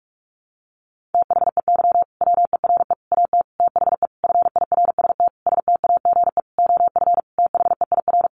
Morse code, heard as a keyed tone, about 44 wpm, 700 Hz. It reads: THEY WERE AT THE FIRST STAGE OF THEIR
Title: Sentences from Top 1200 Words 45wpm